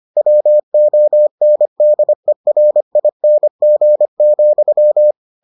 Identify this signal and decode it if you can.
Morse code, heard as a keyed tone, about 25 wpm, 600 Hz. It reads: WONDERING,